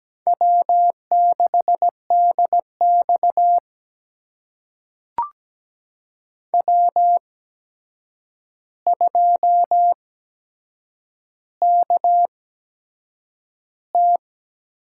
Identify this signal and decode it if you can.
Morse code, heard as a keyed tone, about 17 wpm, spaced as for 7 wpm, 700 Hz. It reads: W6DX W2KT
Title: USA Call Signs - 4 Characters - 8x Character Spacing - 17wpm